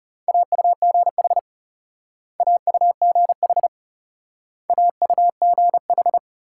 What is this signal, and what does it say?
Morse code, heard as a keyed tone, about 37 wpm, 700 Hz. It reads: AUGH AUGH AUGH
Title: Speed-Racing Multi-letter Phonemes - 30wpm